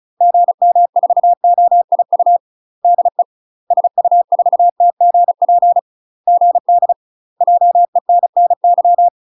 Morse code, heard as a keyed tone, about 34 wpm, 700 Hz. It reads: GM4OIU DE SU4TGP GD JENNY
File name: UK MASTER QSOs TRAINING/35WPM